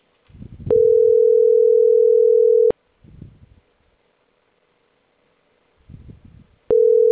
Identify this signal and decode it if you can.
Ringback tone (North America), 440 + 480 Hz.